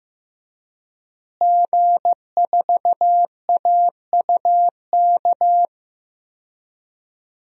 Morse code, heard as a keyed tone, about 15 wpm, 700 Hz. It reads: G4AUK
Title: DXCC-100 Call Signs - 5 Characters - Repeated 3x - 15wpm